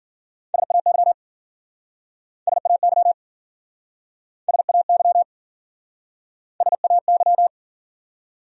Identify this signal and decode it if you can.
Morse code, heard as a keyed tone, about 53 wpm, 700 Hz. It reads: SAY SAY SAY SAY